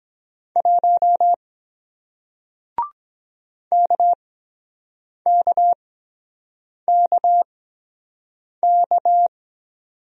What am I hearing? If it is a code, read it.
Morse code, heard as a keyed tone, about 23 wpm, spaced as for 7 wpm, 700 Hz. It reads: 1 KKKK